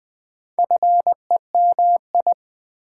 Morse code, heard as a keyed tone, about 20 wpm, 700 Hz. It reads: FEMI
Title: Build A Word - 4 Letters At A Time - 8-Letter Words - 20wpm